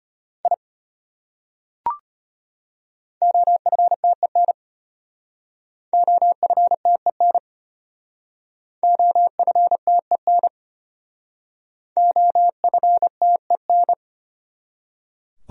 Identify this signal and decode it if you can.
Morse code, heard as a keyed tone, about 30 wpm, 700 Hz. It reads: I OFTEN OFTEN OFTEN OFTEN
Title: Speed-Racing Top 500 Words 25wpm